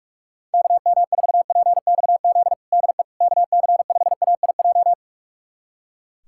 Morse code, heard as a keyed tone, about 43 wpm, 700 Hz. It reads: KM4PXZ DE KC5AIJ
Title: USA Callsign DE Callsign - 6 Characters - 45wpm